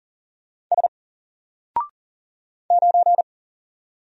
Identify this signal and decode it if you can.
Morse code, heard as a keyed tone, about 40 wpm, 700 Hz. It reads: S 9
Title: Single Character (Letter:Number:Punctuation:Prosign) Alt 40wpm